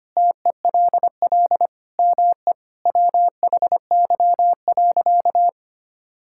Morse code, heard as a keyed tone, about 25 wpm, 700 Hz. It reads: TELL ME WHY.
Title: Sentences from Top 1000 Words - Review Part 6 - 25wpm